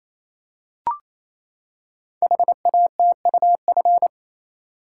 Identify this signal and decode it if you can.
Morse code, heard as a keyed tone, about 28 wpm, 700 Hz. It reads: HATUF
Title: Code Groups - 5 Letters - 28wpm